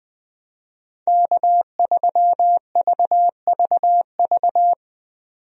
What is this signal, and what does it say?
Morse code, heard as a keyed tone, about 20 wpm, 700 Hz. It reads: K3VVV
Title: USA Call Signs - 5 Characters - 20wpm